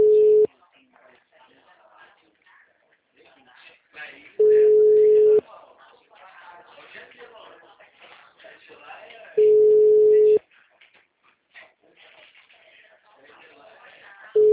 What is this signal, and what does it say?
Ringback tone (Europe, 425 Hz), with speech.